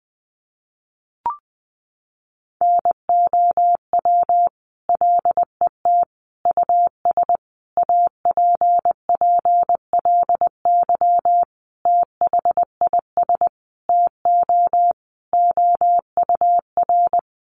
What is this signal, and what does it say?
Morse code, heard as a keyed tone, about 20 wpm, 700 Hz. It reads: NOW LET US APPLY THIS TO OUR
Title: Sentences from Top 1200 Words 20wpm